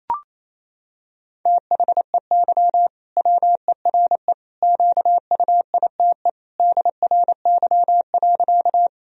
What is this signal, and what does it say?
Morse code, heard as a keyed tone, about 28 wpm, 700 Hz. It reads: THEY WERE QUITE DRY.
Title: Sentences from Top 1000 Words - Review Part 3 - 28wpm